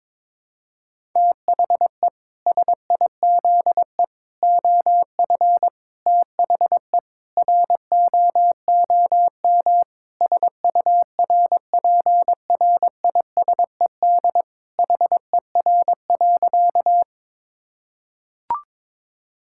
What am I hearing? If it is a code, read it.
Morse code, heard as a keyed tone, about 22 wpm, 700 Hz. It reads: THE SIZE OF THE ROOM SURPRISED HER.
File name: practice / 22wpm / Sentences with 4-Word Phrases - 22wpm